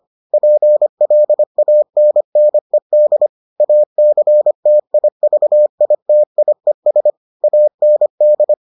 Morse code, heard as a keyed tone, about 25 wpm, 600 Hz. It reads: PLANNED ACTIVITIES AND